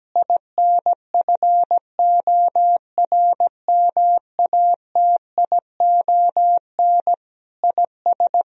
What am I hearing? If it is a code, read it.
Morse code, heard as a keyed tone, about 17 wpm, 700 Hz. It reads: INFORMATION IS
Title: Sentences from Top 1000 Words - Review Part 3 - 17wpm